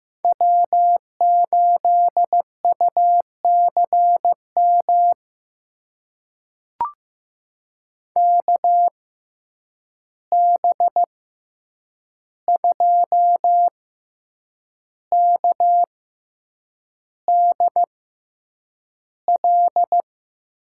Morse code, heard as a keyed tone, about 15 wpm, spaced as for 7 wpm, 700 Hz. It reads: W8UCM KB2KDL
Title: USA Call Signs - 6x Character Spacing - 15wpm